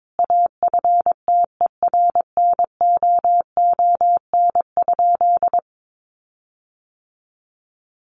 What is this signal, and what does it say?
Morse code, heard as a keyed tone, about 22 wpm, 700 Hz. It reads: AFTERNOON?
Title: Sentences from Top 1500 Words 22wpm